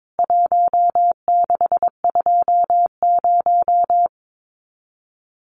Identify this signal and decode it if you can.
Morse code, heard as a keyed tone, about 22 wpm, 700 Hz. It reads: 1620
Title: CWT Exchange 22wpm